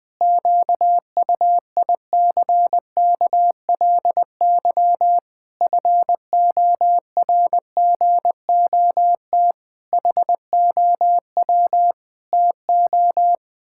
Morse code, heard as a keyed tone, about 21 wpm, 700 Hz. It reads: QUICKLY FORGOT HOW TO
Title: Pangram Challenge v2 - 20wpm